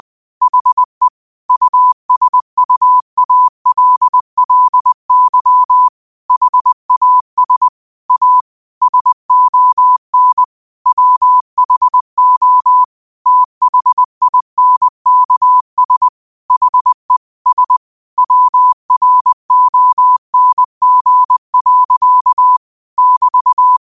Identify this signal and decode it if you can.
Morse code, heard as a keyed tone, about 20 wpm, 1000 Hz. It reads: HE USUALLY HAS A SON WHO THINKS HE S WRONG. <BT>
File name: qotdfiles / 20wpm